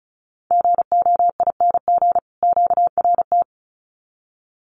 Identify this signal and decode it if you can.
Morse code, heard as a keyed tone, about 34 wpm, 700 Hz. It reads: GOING QRT
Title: Common QSO Phrases 35wpm